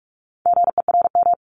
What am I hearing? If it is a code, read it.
Morse code, heard as a keyed tone, about 46 wpm, 700 Hz. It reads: GERM
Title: Multi-letter Phonemes 45wpm